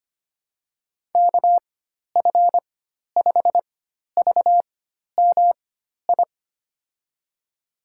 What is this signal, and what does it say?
Morse code, heard as a keyed tone, about 25 wpm, spaced as for 12 wpm, 700 Hz. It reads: KF5VMI
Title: USA Call Signs - 6 Characters - 4x Character Spacing - 25wpm